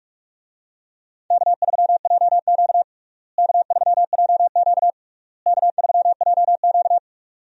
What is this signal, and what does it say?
Morse code, heard as a keyed tone, about 46 wpm, 700 Hz. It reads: K3JQ K3JQ K3JQ
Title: USA Call Signs - 4 Characters - Repeated 3x - 45wpm